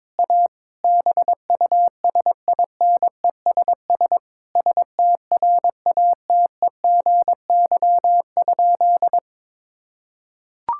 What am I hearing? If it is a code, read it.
Morse code, heard as a keyed tone, about 22 wpm, 700 Hz. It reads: A BUSINESS STRATEGY?
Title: Sentences from Top 1000 Words - Review Part 5 - 22wpm